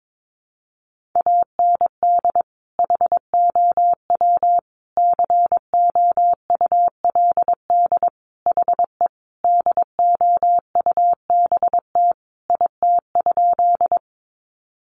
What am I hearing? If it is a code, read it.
Morse code, heard as a keyed tone, about 22 wpm, 700 Hz. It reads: AND HOW COULD HE DOUBT IT?
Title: Sentences from Top 1500 Words 22wpm